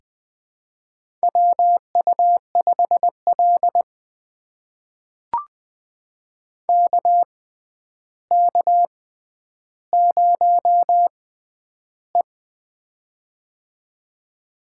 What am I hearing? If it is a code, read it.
Morse code, heard as a keyed tone, about 20 wpm, spaced as for 10 wpm, 700 Hz. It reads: WU5L KK0E
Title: USA Call Signs - 4 Characters - 6x Character Spacing - 20wpm